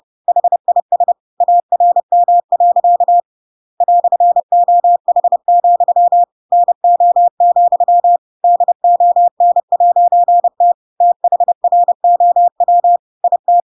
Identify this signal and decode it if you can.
Morse code, heard as a keyed tone, about 30 wpm, 700 Hz. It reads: HIS ARM. "OH, NO, DON'T THROW IT
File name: Pinocchio / Pinocchio ch7 30 30 ik1hge